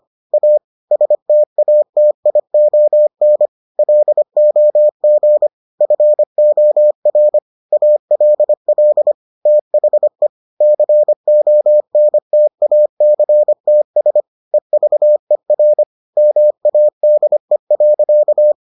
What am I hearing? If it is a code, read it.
Morse code, heard as a keyed tone, about 25 wpm, 600 Hz. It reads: A STATION LOG FOR ALL THE CONTACTS EVER MADE.